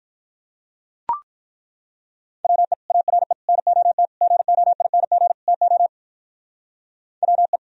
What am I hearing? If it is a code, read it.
Morse code, heard as a keyed tone, about 42 wpm, 700 Hz. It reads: WE ARE NOT GOING TO WE
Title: Speed-Racing - Sets of 5 Words 35wpm